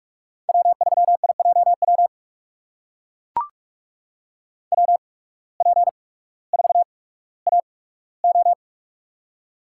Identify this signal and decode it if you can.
Morse code, heard as a keyed tone, about 44 wpm, spaced as for 19 wpm, 690 Hz. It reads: W3IJW WP4AO
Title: USA Call Signs - 8x Character Spacing - 45wpm